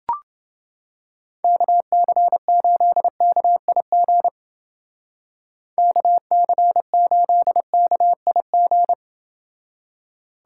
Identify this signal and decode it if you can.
Morse code, heard as a keyed tone, about 29 wpm, 700 Hz. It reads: KC8KIG KC8KIG